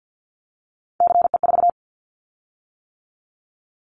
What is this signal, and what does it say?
Morse code, heard as a keyed tone, about 50 wpm, 690 Hz. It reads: CE4
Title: Callsign Prefixes for Top 100 DXCC - 50wpm